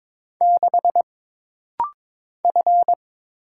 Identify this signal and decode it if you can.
Morse code, heard as a keyed tone, about 22 wpm, 700 Hz. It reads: 6 F
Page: This rapid-fire timing is geared to push your ICR (Instant Character Recognition) skills.